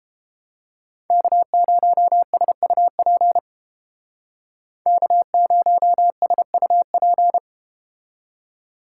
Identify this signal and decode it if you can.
Morse code, heard as a keyed tone, about 31 wpm, 700 Hz. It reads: K0SUP K0SUP